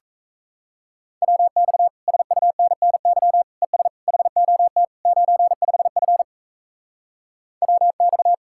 Morse code, heard as a keyed tone, about 37 wpm, 700 Hz. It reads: WX SUNNY ES HOT 95F WX